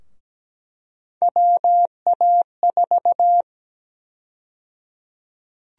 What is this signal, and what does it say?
Morse code, heard as a keyed tone, about 17 wpm, 700 Hz. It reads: WA4
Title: USA Call Sign Prefixes - Repeated 2x - 17wpm